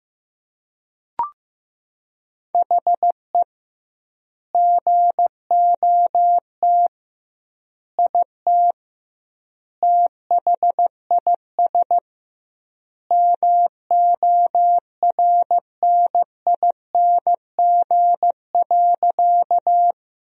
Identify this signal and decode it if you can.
Morse code, heard as a keyed tone, about 15 wpm, 700 Hz. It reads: HE GOT IT THIS MORNING.
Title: 2x Word-Spacing - Sentences from Top 400 Words - 15wpm